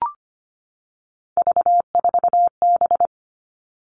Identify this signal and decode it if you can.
Morse code, heard as a keyed tone, about 25 wpm, 700 Hz. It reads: V4B
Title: All Cut Numbers - 3-Digit 25wpm